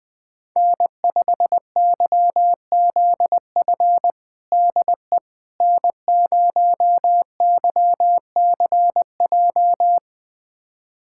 Morse code, heard as a keyed tone, about 20 wpm, 700 Hz. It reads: N5YZF DE N0YCJ
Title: USA Callsign DE Callsign - 5 Characters - 20wpm